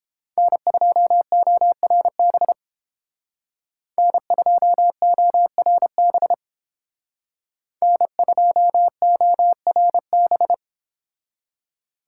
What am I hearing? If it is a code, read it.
Morse code, heard as a keyed tone, about 29 wpm, 700 Hz. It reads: N2ORB N2ORB N2ORB